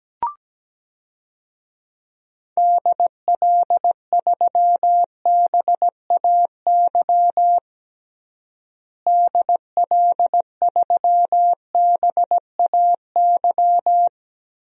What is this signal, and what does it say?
Morse code, heard as a keyed tone, about 17 wpm, 700 Hz. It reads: DL3BAY DL3BAY
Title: DXCC-100 Call Signs - 6 Characters - Repeated 3x - 17wpm